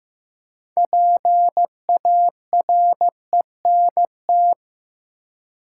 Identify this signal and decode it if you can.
Morse code, heard as a keyed tone, about 15 wpm, 700 Hz. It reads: PARENT
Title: Six Letter Words - 15wpm